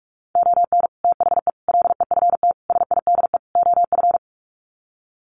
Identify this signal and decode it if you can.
Morse code, heard as a keyed tone, about 44 wpm, 700 Hz. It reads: ON THE LEFT SIDE OF
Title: Sets of 5 Words Encore2 45wpm